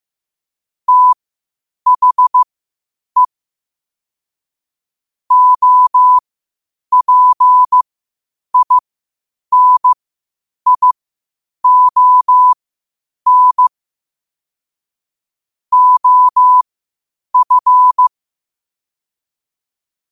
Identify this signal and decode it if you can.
Morse code, heard as a keyed tone, about 15 wpm, spaced as for 8 wpm, 1000 Hz. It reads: THE OPINION OF
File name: qotdfiles / 8wpm